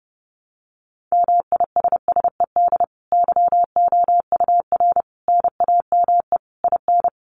Morse code, heard as a keyed tone, about 28 wpm, 700 Hz. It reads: MISSED YOUR NAME IN
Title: Common QSO Phrases 30wpm